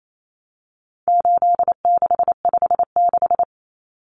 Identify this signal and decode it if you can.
Morse code, heard as a keyed tone, about 28 wpm, 700 Hz. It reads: 8656
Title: Common Cut Numbers - 4-Digit 28wpm